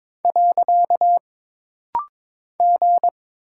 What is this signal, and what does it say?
Morse code, heard as a keyed tone, about 22 wpm, 700 Hz. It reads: . G
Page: This rapid-fire timing is geared to push your ICR (Instant Character Recognition) skills.